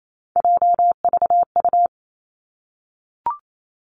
Morse code, heard as a keyed tone, about 28 wpm, 700 Hz. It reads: JVU
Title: Three Challenging Letter-Number Combinations 28wpm